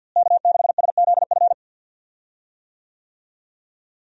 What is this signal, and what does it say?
Morse code, heard as a keyed tone, about 51 wpm, 700 Hz. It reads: K6SZF
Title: USA Call Signs - 5 Characters - Repeated 3x - 50wpm